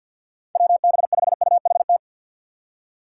Morse code, heard as a keyed tone, about 51 wpm, 690 Hz. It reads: WB5UHT
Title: USA Call Signs - 6 Characters - 3x Character Spacing - 50wpm